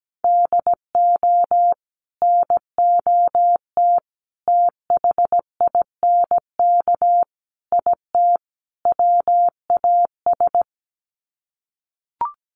Morse code, heard as a keyed tone, about 17 wpm, 700 Hz. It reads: DO NOT THINK IT WAS
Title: Sets of 5 Words Encore 17wpm